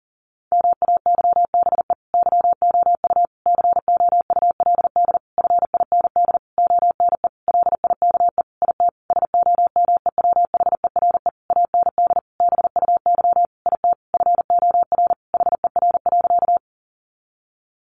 Morse code, heard as a keyed tone, about 40 wpm, 700 Hz. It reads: MAYBE YOU COULD FIND ONE LIKE IT SOMEWHERE AND BUY IT FOR HER.
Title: Sentences from Top 1400 Words 40wpm